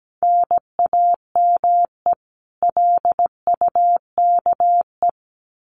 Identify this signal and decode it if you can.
Morse code, heard as a keyed tone, about 17 wpm, 700 Hz. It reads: NAME LUKE
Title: Common QSO Phrases 17wpm